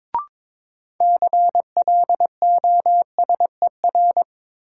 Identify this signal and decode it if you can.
Morse code, heard as a keyed tone, about 22 wpm, 700 Hz. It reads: CLOSER
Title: Sets of 5 Words Encore - New Word Drill - 22wpm